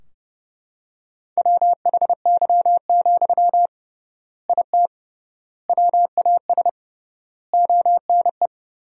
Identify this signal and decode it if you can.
Morse code, heard as a keyed tone, about 31 wpm, 700 Hz. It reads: WHY, IT WAS ONE
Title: 3x Word-Spacing - Sentences from Top 300 Words - 30wpm